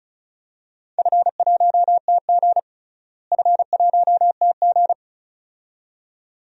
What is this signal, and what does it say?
Morse code, heard as a keyed tone, about 36 wpm, 700 Hz. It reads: F1TG F1TG
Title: DXCC-100 Call Signs - 4 Characters - Repeated 2x - 35wpm